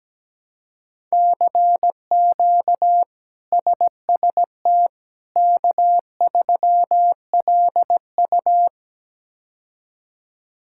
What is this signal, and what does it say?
Morse code, heard as a keyed tone, about 17 wpm, 700 Hz. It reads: CQ SST K3LU
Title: K1USN SST QSO Practice - 17wpm